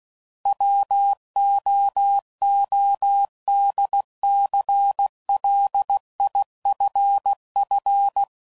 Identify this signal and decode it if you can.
Morse code, heard as a keyed tone, about 16 wpm, 800 Hz. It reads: WOODCLIFF
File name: CW Code Groups @15WPM / fromSM's / Cities